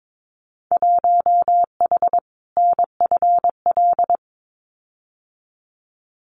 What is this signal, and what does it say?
Morse code, heard as a keyed tone, about 22 wpm, 700 Hz. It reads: 1H NFL
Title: Winter Field Day Exchanges 22wpm